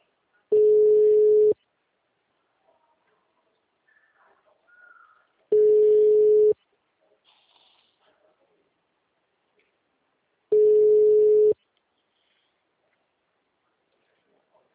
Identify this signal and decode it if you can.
Ringback tone (Europe, 425 Hz).